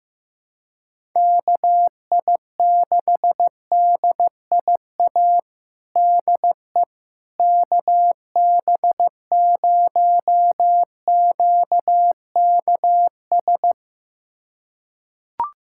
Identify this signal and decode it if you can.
Morse code, heard as a keyed tone, about 15 wpm, 700 Hz. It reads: KI6DIA DE KB0QKS
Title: USA Callsign DE Callsign 15wpm